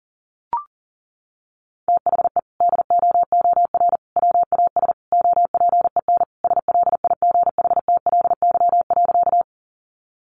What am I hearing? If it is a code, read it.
Morse code, heard as a keyed tone, about 40 wpm, 700 Hz. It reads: THE DOOR WAS OPEN SLIGHTLY.
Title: Sentences from Top 1300 Words 40wpm